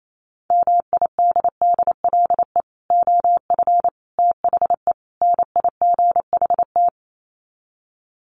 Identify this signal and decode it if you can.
Morse code, heard as a keyed tone, about 28 wpm, 700 Hz. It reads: MIDDLE OF THE NIGHT
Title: Sets of 4 Words 28wpm